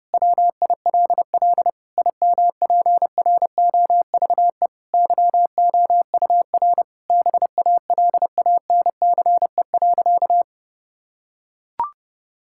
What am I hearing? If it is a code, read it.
Morse code, heard as a keyed tone, about 30 wpm, 700 Hz. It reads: WILL IMPROVE YOUR BALANCE.
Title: Sentences from Top 1000 Words - Review Part 6 - 30wpm